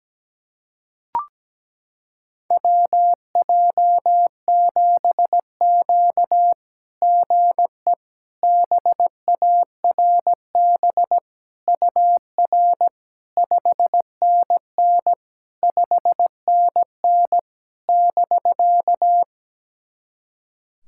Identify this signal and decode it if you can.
Morse code, heard as a keyed tone, about 17 wpm, 700 Hz. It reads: WJ7Q GE BARB UR 5NN 5NN <BK>
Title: SOTA QSO Practice 17wpm